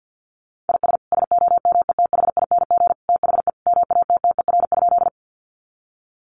Morse code, heard as a keyed tone, about 50 wpm, 700 Hz. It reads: IS SOMETHING THE MATTER?
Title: Sentences from Top 700 Words 50wpm